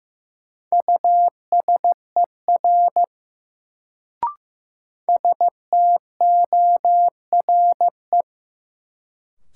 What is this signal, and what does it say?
Morse code, heard as a keyed tone, about 15 wpm, 700 Hz. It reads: USER STORE
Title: Sentences from Top 1000 Words - Review Part 5 - New Word Drill - 15wpm